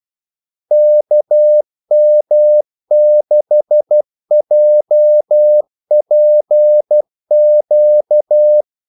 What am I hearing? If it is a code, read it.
Morse code, heard as a keyed tone, about 12 wpm, 600 Hz. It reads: KM6JPQ
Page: Kaldesignaler 12 wpm